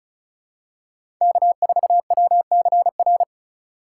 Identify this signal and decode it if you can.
Morse code, heard as a keyed tone, about 34 wpm, 700 Hz. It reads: K4WCR
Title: USA Call Signs - 3x Character Spacing - 35wpm